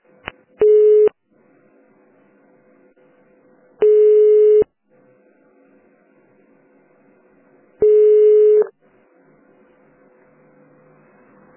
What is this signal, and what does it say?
Ringback tone (425 Hz, 1 s on, 3 s off).